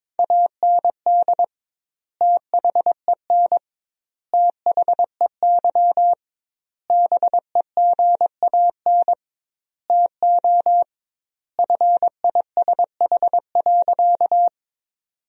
Morse code, heard as a keyed tone, about 22 wpm, 700 Hz. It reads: AND THEN THEY BEGAN TO FISH.
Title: 2x Word-Spacing - Sentences from Top 300 Words - 22wpm